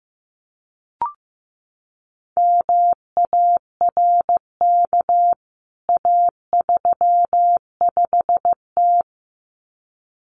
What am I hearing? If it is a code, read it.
Morse code, heard as a keyed tone, about 15 wpm, 700 Hz. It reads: MARK A35T
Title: CWT Exchange 15wpm